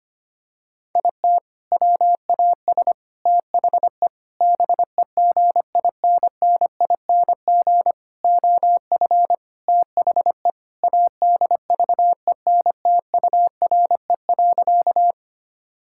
Morse code, heard as a keyed tone, about 25 wpm, 700 Hz. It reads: IT WAS THE BEGINNING OF THE ADVENTURE.
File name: practice / 25wpm / Sentences with 4-Word Phrases - 25wpm